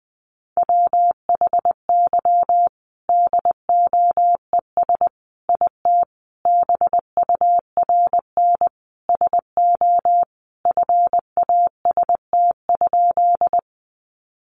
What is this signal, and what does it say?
Morse code, heard as a keyed tone, about 20 wpm, 700 Hz. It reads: WHY DOES IT BURN SO FAST?
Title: Sentences from Top 1300 Words 20wpm